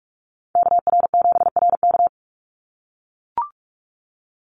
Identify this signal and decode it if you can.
Morse code, heard as a keyed tone, about 44 wpm, 700 Hz. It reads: KR7RK
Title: NA SOTA Chaser Callsigns 45wpm